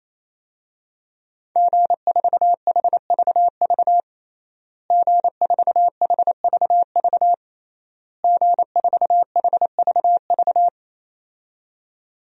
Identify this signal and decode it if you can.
Morse code, heard as a keyed tone, about 28 wpm, 700 Hz. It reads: G4HVV G4HVV G4HVV
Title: DXCC-100 Call Signs - 5 Characters - Repeated 3x - 28wpm